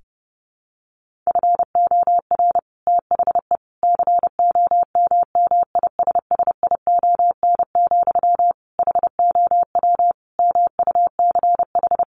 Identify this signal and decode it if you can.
Morse code, heard as a keyed tone, about 30 wpm, 700 Hz. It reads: FOR THE COMMISSION, HOW MUCH
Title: Sentences from Top 1400 Words 30wpm